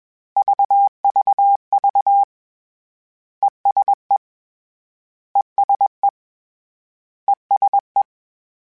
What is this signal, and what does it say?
Morse code, heard as a keyed tone, about 21 wpm, 800 Hz. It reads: VVV ESE ESE ESE
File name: General / General-mp3s / ESE ING @20